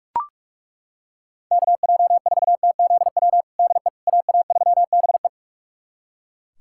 Morse code, heard as a keyed tone, about 42 wpm, 700 Hz. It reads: KJ4TZW DE AA3BE
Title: USA Callsign DE Callsign 45wpm